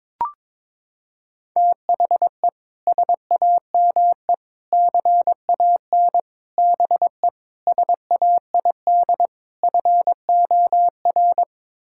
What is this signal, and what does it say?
Morse code, heard as a keyed tone, about 22 wpm, 700 Hz. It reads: THE SAME CAN BE SAID FOR
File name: practice / 22wpm / Sets of 6 Words - 22wpm